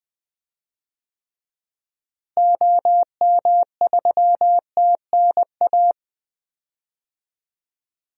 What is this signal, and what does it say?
Morse code, heard as a keyed tone, about 20 wpm, 700 Hz. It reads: OM3TNA
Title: DXCC-100 Call Signs - 6 Characters - Repeated 3x - 20wpm